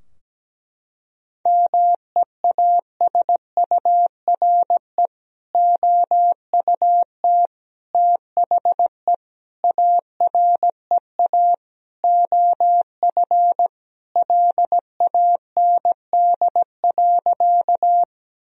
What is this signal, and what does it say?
Morse code, heard as a keyed tone, about 17 wpm, 700 Hz. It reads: MEASURE OUT THE AREA OF LAND.
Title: Sentences from Top 1000 Words - Review Part 2 - 17wpm